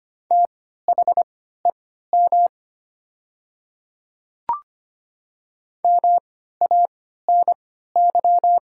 Morse code, heard as a keyed tone, about 25 wpm, spaced as for 14 wpm, 700 Hz. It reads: THEM MANY
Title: 3x Character Spacing - Top 100 Words - 25wpm